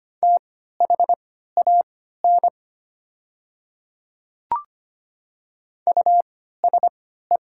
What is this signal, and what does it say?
Morse code, heard as a keyed tone, about 25 wpm, spaced as for 14 wpm, 690 Hz. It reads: THAN USE
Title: 3x Character Spacing - Top 300 Words - 25wpm